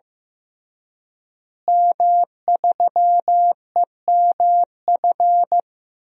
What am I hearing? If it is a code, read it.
Morse code, heard as a keyed tone, about 15 wpm, 700 Hz. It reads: M3EMF
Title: DXCC-100 Call Signs - 5 Characters - Repeated 3x - 15wpm